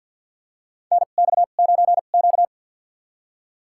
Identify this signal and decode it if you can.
Morse code, heard as a keyed tone, about 45 wpm, spaced as for 34 wpm, 700 Hz. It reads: NX9Q
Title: USA Call Signs - 4 Characters - 2x Character Spacing - 50wpm